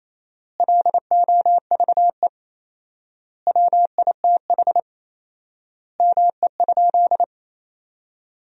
Morse code, heard as a keyed tone, about 28 wpm, 700 Hz. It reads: LOVE WITH ME?
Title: 4x Word-Spacing - Sentences from Top 400 Words - 28wpm